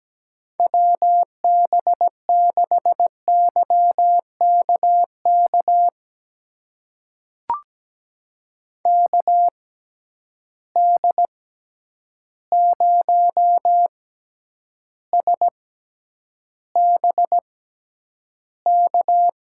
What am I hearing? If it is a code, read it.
Morse code, heard as a keyed tone, about 17 wpm, spaced as for 9 wpm, 700 Hz. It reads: WB6YKK KD0SBK
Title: USA Call Signs - 6 Characters - 6x Character Spacing - 17wpm